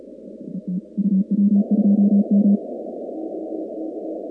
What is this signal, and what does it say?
Morse code, heard as a keyed tone, about 36 wpm, 200 Hz. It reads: ET UW2M
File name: contests / 2019-cw-wpx-kt5j-k5pi- -2330 0-38